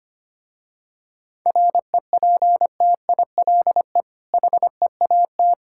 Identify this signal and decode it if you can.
Morse code, heard as a keyed tone, about 25 wpm, 700 Hz. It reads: REPTILE HEAT
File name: practice / 25wpm / Difficult Words with Hints - 25wpm